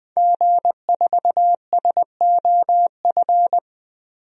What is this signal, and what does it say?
Morse code, heard as a keyed tone, about 21 wpm, 700 Hz. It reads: G4SOF
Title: DXCC-100 Call Signs - 5 Characters - Repeated 2x - 20wpm